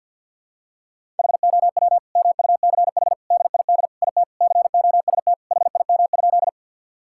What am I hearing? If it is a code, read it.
Morse code, heard as a keyed tone, about 52 wpm, 700 Hz. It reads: HOW MUCH DID IT COST HIM?
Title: Sentences from Top 1000 Words - Review Part 5 - 50wpm